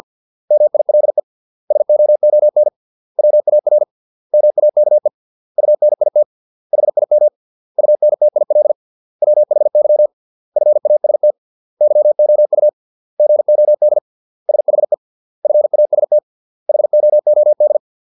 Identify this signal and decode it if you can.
Morse code, heard as a keyed tone, about 50 wpm, 600 Hz. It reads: MILE SOON WAR MAKE UNIT HIM UNTIL WHY FAST YOU GOD SHE FAST HOOD